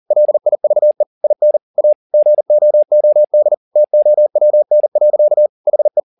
Morse code, heard as a keyed tone, about 40 wpm, 600 Hz. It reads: LIVE IN A GOOD TOWN. HE
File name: the old man and the sea 2 40wpm